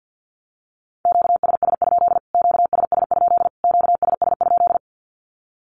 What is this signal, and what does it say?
Morse code, heard as a keyed tone, about 50 wpm, 700 Hz. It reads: QSS? QSS? QSS?
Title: Q-Codes Repeated 3x 50wpm